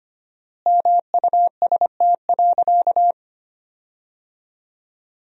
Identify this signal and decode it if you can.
Morse code, heard as a keyed tone, about 25 wpm, 700 Hz. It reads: MUST.
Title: 4x Word-Spacing - Sentences from Top 200 Words - 25wpm